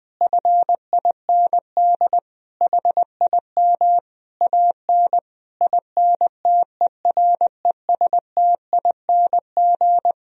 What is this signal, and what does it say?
Morse code, heard as a keyed tone, about 20 wpm, 700 Hz. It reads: FIND HIM AN INTERESTING
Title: Sentences from Top 1000 Words - Review Part 1 - 20wpm